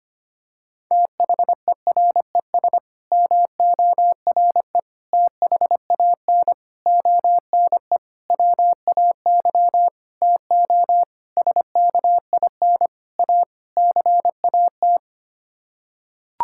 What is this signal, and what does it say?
Morse code, heard as a keyed tone, about 25 wpm, 700 Hz. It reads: THERES MORE THAN ONE WAY TO SKIN A CAT
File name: Sets of 9 Words - 25wpm